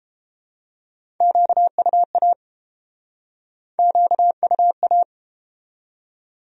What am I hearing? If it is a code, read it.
Morse code, heard as a keyed tone, about 30 wpm, 700 Hz. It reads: QUA QUA